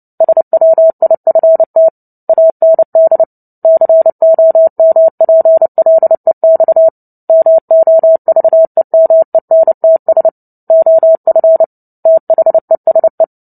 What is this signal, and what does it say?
Morse code, heard as a keyed tone, about 30 wpm, 650 Hz. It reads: SWIFT AND COMPLEX MOVEMENTS OF THESE
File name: Book Two / 03-Chapter 3 - 29 WPM